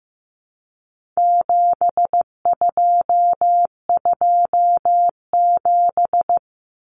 Morse code, heard as a keyed tone, about 15 wpm, 700 Hz. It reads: 7227
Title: Common Cut Numbers - 4-Digit 15wpm